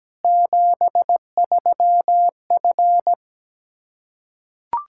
Morse code, heard as a keyed tone, about 17 wpm, 700 Hz. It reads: 73F
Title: Practice QSO - Exchange 2 - Info - 17wpm